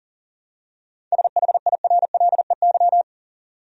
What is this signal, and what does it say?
Morse code, heard as a keyed tone, about 40 wpm, 700 Hz. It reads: SHIRLEY
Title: USA Common Names - 40wpm